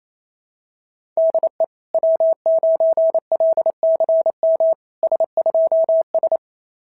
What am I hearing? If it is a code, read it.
Morse code, heard as a keyed tone, about 27 wpm, 660 Hz. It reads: DE W9LCM S2S
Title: SOTA QSO Practice 28wpm